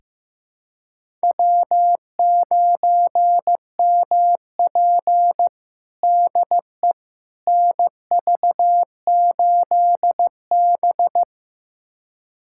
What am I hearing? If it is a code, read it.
Morse code, heard as a keyed tone, about 16 wpm, 700 Hz. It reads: W9MP DE NV8B
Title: USA Callsign DE Callsign - 4 Characters - 15wpm